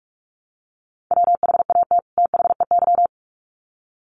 Morse code, heard as a keyed tone, about 44 wpm, 700 Hz. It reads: WHAT THEY
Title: Sets of 2 Words 45wpm